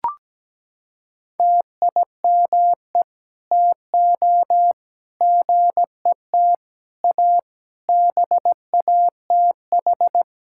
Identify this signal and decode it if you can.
Morse code, heard as a keyed tone, about 17 wpm, 700 Hz. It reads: TIME TO GET A BATH
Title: Sentences from Top 1600 Words 17wpm